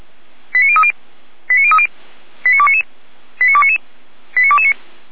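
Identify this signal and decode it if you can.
Five-tone selective call (ZVEI), heard as quick runs of five tones, tones of about 70 ms. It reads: 89030 89030 89299 89299 89229
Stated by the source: ZVEI five-tone selective call received by SDR, from the Feuerwehr Orlamünde (fire brigade) audio archive.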